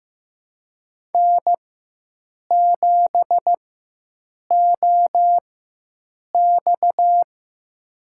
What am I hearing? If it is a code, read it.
Morse code, heard as a keyed tone, about 15 wpm, spaced as for 7 wpm, 700 Hz. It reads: N7OX
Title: USA Call Signs - 4x Character Spacing - 15wpm